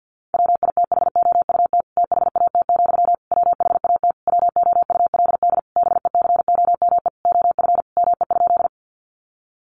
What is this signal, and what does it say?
Morse code, heard as a keyed tone, about 50 wpm, 700 Hz. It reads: WITHOUT THAT, WHAT WOULD BECOME OF ME?
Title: Sentences from Top 600 Words 50wpm